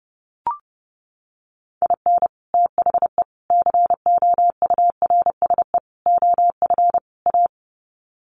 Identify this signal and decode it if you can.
Morse code, heard as a keyed tone, about 30 wpm, 700 Hz. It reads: IN THE COURSE OF A
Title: Sets of 5 Words Encore2 30wpm